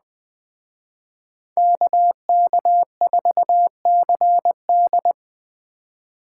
Morse code, heard as a keyed tone, about 20 wpm, 700 Hz. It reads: KK4CD
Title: Speed Racing - USA Call Signs 17wpm